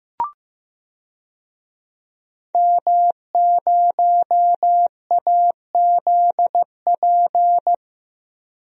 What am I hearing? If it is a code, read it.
Morse code, heard as a keyed tone, about 15 wpm, 700 Hz. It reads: M0AZP
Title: DXCC-100 Call Signs - 5 Characters - 15wpm